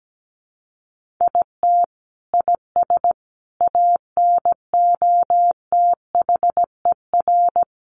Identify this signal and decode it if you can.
Morse code, heard as a keyed tone, about 17 wpm, 700 Hz. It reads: IT IS ANOTHER
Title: Sentences from Top 1400 Words 17wpm